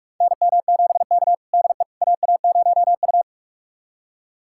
Morse code, heard as a keyed tone, about 46 wpm, 700 Hz. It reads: NM7K DE AA0U
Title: USA Callsign DE Callsign - 4 Characters - 45wpm